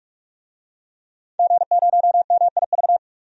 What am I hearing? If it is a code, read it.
Morse code, heard as a keyed tone, about 44 wpm, 700 Hz. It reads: G0MIV
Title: DXCC-100 Call Signs - 5 Characters - 45wpm